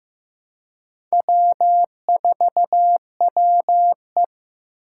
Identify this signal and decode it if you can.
Morse code, heard as a keyed tone, about 15 wpm, 700 Hz. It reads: W4WE
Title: Four Challenging Letter-Number Combinations 15wpm